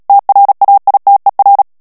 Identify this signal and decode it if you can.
Morse code, heard as a keyed tone, about 37 wpm, 800 Hz. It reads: TRAITER
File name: CW / FICHIERS-MP3 / FRANCAIS TEXTE CLAIR 2 VITESSE 35 MOTS MINUTES